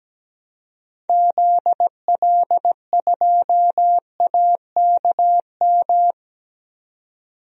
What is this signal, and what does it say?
Morse code, heard as a keyed tone, about 17 wpm, 700 Hz. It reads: ZL2AKM
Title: DXCC-100 Call Signs - 6 Characters - 17wpm